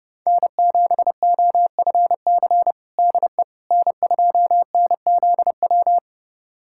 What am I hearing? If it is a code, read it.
Morse code, heard as a keyed tone, about 30 wpm, 700 Hz. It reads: N7OFC DE N2NZW
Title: USA Callsign DE Callsign - 5 Characters - 30wpm